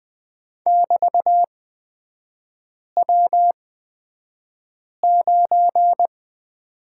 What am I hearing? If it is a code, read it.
Morse code, heard as a keyed tone, about 20 wpm, spaced as for 5 wpm, 700 Hz. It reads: <BT>W9
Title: 224 Lesson 36e - BT - Three Characters 20wpm